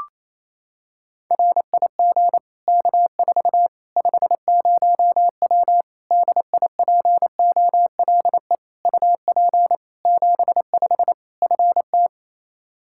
Morse code, heard as a keyed tone, about 27 wpm, 700 Hz. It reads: RIG K4 50W DIPOLE UP 75 FT
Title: Practice QSO - Exchange 2 - Info - 28wpm